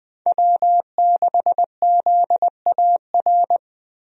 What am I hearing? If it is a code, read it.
Morse code, heard as a keyed tone, about 20 wpm, 700 Hz. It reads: W6ZAR
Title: USA Call Signs - 5 Characters - Repeated 2x - 20wpm